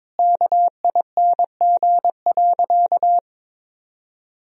Morse code, heard as a keyed tone, about 22 wpm, 700 Hz. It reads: KING.
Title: Sentences from Top 1000 Words - Review Part 1 - 22wpm